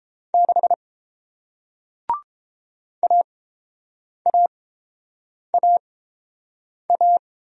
Morse code, heard as a keyed tone, about 30 wpm, spaced as for 7 wpm, 700 Hz. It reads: 6 AAAA